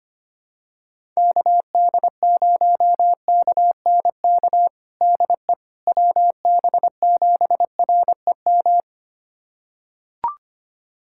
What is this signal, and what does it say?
Morse code, heard as a keyed tone, about 25 wpm, 700 Hz. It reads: KD0KNK DE WB7REM
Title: USA Callsign DE Callsign - 6 Characters - 25wpm